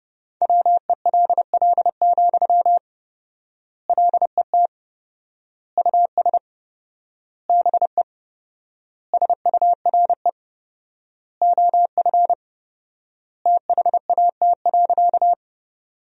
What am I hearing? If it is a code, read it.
Morse code, heard as a keyed tone, about 30 wpm, 700 Hz. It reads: WELL, LET US BE SURE OF THAT.
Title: 4x Word-Spacing - Sentences from Top 300 Words - 30wpm